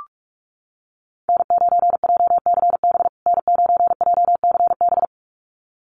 Morse code, heard as a keyed tone, about 44 wpm, 700 Hz. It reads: N9JCB N9JCB
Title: USA Call Signs Repeated 2x 45wpm